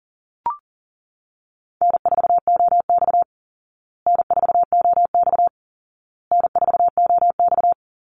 Morse code, heard as a keyed tone, about 40 wpm, 700 Hz. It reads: N4OX N4OX N4OX
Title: Easy Call Signs Repeated 3x - 40wpm